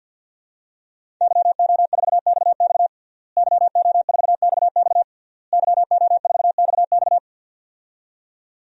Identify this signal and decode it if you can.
Morse code, heard as a keyed tone, about 51 wpm, 700 Hz. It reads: YO4XX YO4XX YO4XX
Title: DXCC-100 Call Signs - 5 Characters - Repeated 3x - 50wpm